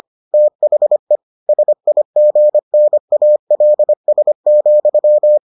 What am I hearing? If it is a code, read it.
Morse code, heard as a keyed tone, about 25 wpm, 600 Hz. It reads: THE SIGNALS,